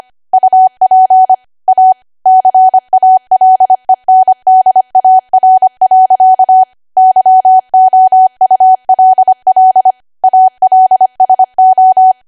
Morse code, heard as a keyed tone, about 25 wpm, 750 Hz. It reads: UP A CALENDAR. YOULL ALSO
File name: cwa / W1AW-Level3 / 25WPM